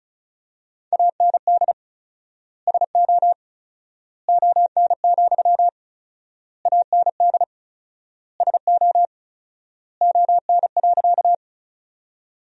Morse code, heard as a keyed tone, about 34 wpm, 700 Hz. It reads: AND SO ON, AND SO ON.
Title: 4x Word-Spacing - Sentences from Top 100 Words - 35wpm